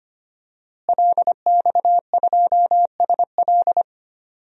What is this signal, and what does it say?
Morse code, heard as a keyed tone, about 25 wpm, 700 Hz. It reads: LX2SL
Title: DXCC-100 Call Signs - 5 Characters - Repeated 3x - 25wpm